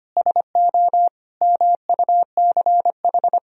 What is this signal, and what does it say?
Morse code, heard as a keyed tone, about 25 wpm, 700 Hz. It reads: SO MUCH
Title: Sentences from Top 1000 Words - Review Part 4 - 25wpm